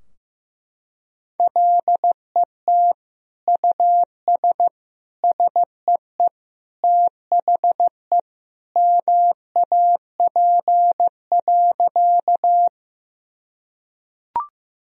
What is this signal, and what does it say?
Morse code, heard as a keyed tone, about 16 wpm, 700 Hz. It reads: LET US SEE THE MAP.
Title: Sentences from Top 1000 Words - Review Part 2 - 15wpm